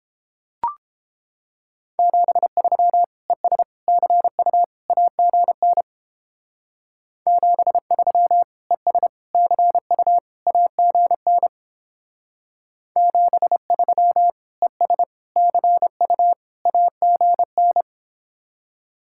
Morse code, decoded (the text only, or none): 73 ES CU AGN 73 ES CU AGN 73 ES CU AGN